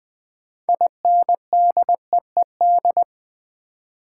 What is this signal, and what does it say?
Morse code, heard as a keyed tone, about 20 wpm, 700 Hz. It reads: INDEED
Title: Build A Word - 2 Letters At A Time - 6-Letter Words - 20wpm